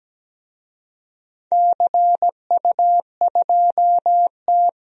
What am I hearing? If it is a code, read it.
Morse code, heard as a keyed tone, about 17 wpm, 700 Hz. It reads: CU2T
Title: DXCC-100 Call Signs - 4 Characters - Repeated 3x - 17wpm